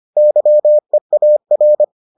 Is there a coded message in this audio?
Yes: Morse code from 0.2 to 1.8 s, heard as a keyed tone, about 25 wpm, 600 Hz. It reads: YEAR